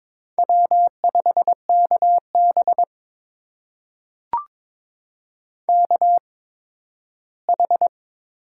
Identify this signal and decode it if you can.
Morse code, heard as a keyed tone, about 22 wpm, spaced as for 13 wpm, 700 Hz. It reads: W5KB KH
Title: USA Call Signs - 4 Characters - 8x Character Spacing - 22wpm